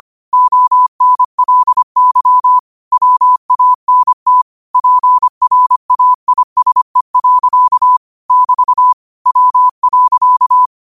Morse code, heard as a keyed tone, about 25 wpm, 1000 Hz. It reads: ONLY WANT PRAISE. <BT> W.
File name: qotdfiles / 25wpm